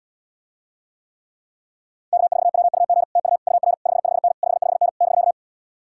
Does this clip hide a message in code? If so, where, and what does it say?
Yes: Morse code from 2.1 to 5.3 s, heard as a keyed tone, about 80 wpm, 700 Hz. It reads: K4WUG TU UR 55N 55N <BK>